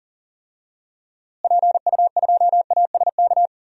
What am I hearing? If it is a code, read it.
Morse code, heard as a keyed tone, about 40 wpm, 700 Hz. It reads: PU2ASK
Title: DXCC-100 Call Signs - 6 Characters - Repeated 3x - 40wpm